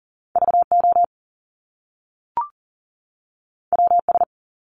Morse code, heard as a keyed tone, about 40 wpm, 700 Hz. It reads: VO WS
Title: Top Two Letter Combinations in Words 40wpm